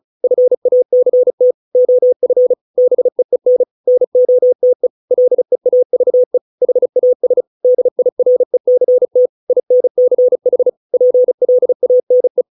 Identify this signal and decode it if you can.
Morse code, heard as a keyed tone, about 35 wpm, 500 Hz. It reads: FACT OF BEEN NOTE LEAVE HAS DIRECT INCH PLANE